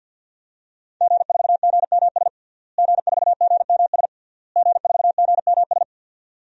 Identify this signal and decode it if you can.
Morse code, heard as a keyed tone, about 51 wpm, 700 Hz. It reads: G4GMS G4GMS G4GMS
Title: DXCC-100 Call Signs - 5 Characters - Repeated 3x - 50wpm